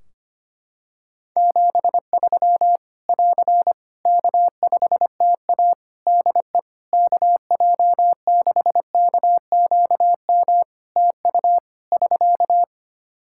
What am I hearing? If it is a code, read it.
Morse code, heard as a keyed tone, about 25 wpm, 700 Hz. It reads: 73 <AR> K5TA DE KJ6KQM TU <SK>
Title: Practice QSO - Exchange 4 - Ending - 25wpm